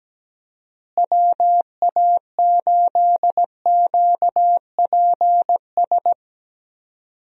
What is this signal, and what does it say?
Morse code, heard as a keyed tone, about 17 wpm, 700 Hz. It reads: WA8QPS
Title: USA Call Signs - 6 Characters - Repeated 3x - 17wpm